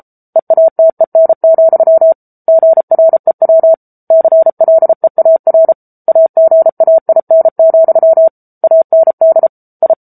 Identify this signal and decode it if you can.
Morse code, heard as a keyed tone, about 34 wpm, 650 Hz. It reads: EATEN, GREW CLEAR AGAIN, AND I